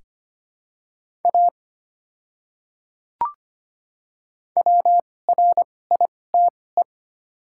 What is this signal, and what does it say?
Morse code, heard as a keyed tone, about 25 wpm, spaced as for 18 wpm, 700 Hz. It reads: A WRITE
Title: 2x Character Spacing - Top 100 Words - 25wpm